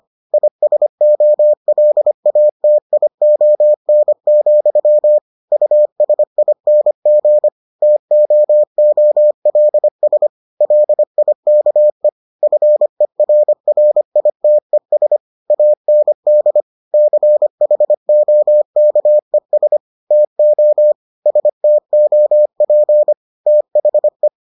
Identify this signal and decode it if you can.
Morse code, heard as a keyed tone, about 25 wpm, 600 Hz. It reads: ISOLATION, USING TOOLS LIKE FERRITES AND CHOKES TO STOP THE